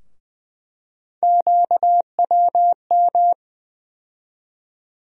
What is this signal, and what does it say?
Morse code, heard as a keyed tone, about 20 wpm, 700 Hz. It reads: QWM
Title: USA Call Sign Postfixes - 3 Letters 20wpm